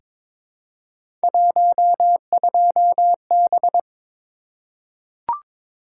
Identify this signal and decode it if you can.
Morse code, heard as a keyed tone, about 22 wpm, 700 Hz. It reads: 12B
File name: practice / 22wpm / Three Letter-Number Combinations - 22wpm